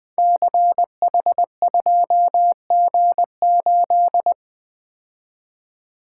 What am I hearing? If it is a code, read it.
Morse code, heard as a keyed tone, about 20 wpm, 700 Hz. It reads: CH2G8
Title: Code Groups - 5 Letter:Numbers - 20wpm